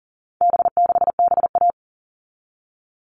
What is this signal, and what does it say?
Morse code, heard as a keyed tone, about 40 wpm, 700 Hz. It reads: B6BA
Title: All Cut Numbers - 4-Digit 40wpm